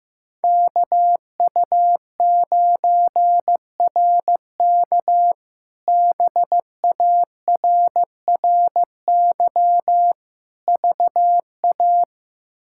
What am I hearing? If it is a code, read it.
Morse code, heard as a keyed tone, about 15 wpm, 700 Hz. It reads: KU9RK BARRY VA
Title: K1USN SST QSO Practice - 15wpm